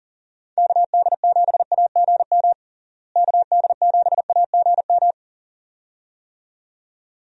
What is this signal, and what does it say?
Morse code, heard as a keyed tone, about 40 wpm, 700 Hz. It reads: KD7AGM KD7AGM
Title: USA Call Signs - 6 Characters - Repeated 2x - 40wpm